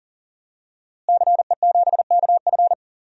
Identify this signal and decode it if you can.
Morse code, heard as a keyed tone, about 40 wpm, 700 Hz. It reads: CE7KF
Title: DXCC-100 Call Signs - 5 Characters - Repeated 3x - 40wpm